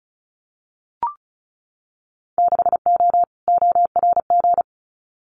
Morse code, heard as a keyed tone, about 34 wpm, 700 Hz. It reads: 6O ORG
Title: Winter Field Day Exchanges Encore 35wpm